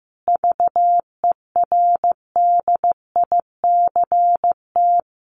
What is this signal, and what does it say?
Morse code, heard as a keyed tone, about 14 wpm, 700 Hz. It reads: VERDICT
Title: A Sound Of Thunder 15wpm